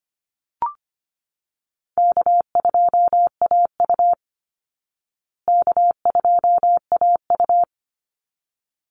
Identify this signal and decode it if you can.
Morse code, heard as a keyed tone, about 25 wpm, 700 Hz. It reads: K2AU K2AU
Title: Call Signs Repeated 3x - 25wpm